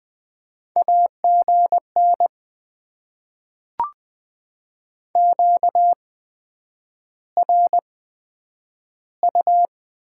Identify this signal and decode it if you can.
Morse code, heard as a keyed tone, about 20 wpm, spaced as for 9 wpm, 700 Hz. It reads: AGN QRU
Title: Top 90 QSO Elements - 8x Character Spacing - 20wpm